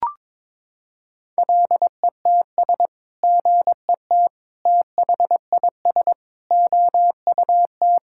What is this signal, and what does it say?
Morse code, heard as a keyed tone, about 22 wpm, 700 Hz. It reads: LETS GET THIS OUT
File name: practice / 22wpm / Sets of 7 Words - 22wpm